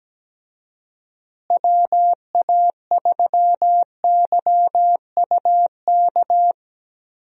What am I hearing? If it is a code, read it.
Morse code, heard as a keyed tone, about 17 wpm, 700 Hz. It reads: WA3YUK
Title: USA Call Signs - 6 Characters - Repeated 2x - 17wpm